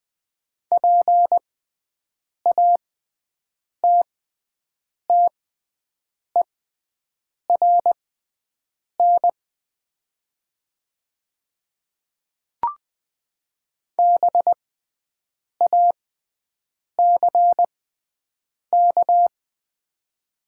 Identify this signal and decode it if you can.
Morse code, heard as a keyed tone, about 20 wpm, spaced as for 7 wpm, 700 Hz. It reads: PATTERN BACK
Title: 6x Character Spacing - Top 500 Words - 20wpm